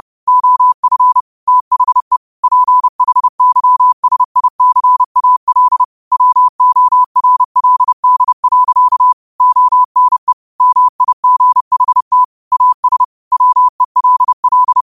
Morse code, heard as a keyed tone, about 30 wpm, 1000 Hz. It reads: OF THE PHYSICAL WORLD. ONE MIGHT AS WELL